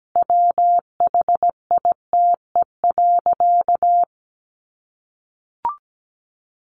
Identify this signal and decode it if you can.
Morse code, heard as a keyed tone, about 17 wpm, 700 Hz. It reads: WHITE.
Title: Sentences from Top 1500 Words 17wpm